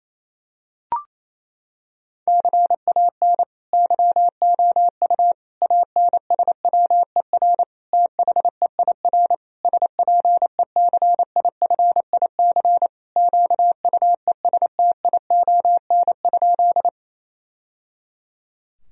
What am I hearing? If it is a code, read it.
Morse code, heard as a keyed tone, about 28 wpm, 700 Hz. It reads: CAN YOU ANSWER THEIR SPECIFIC QUESTION?
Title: Sentences from Top 1000 Words - Review Part 3 - 28wpm